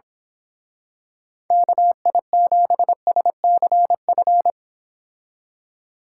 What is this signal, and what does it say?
Morse code, heard as a keyed tone, about 26 wpm, 690 Hz. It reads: KI7SCF
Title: Speed Racing - USA Call Signs 22wpm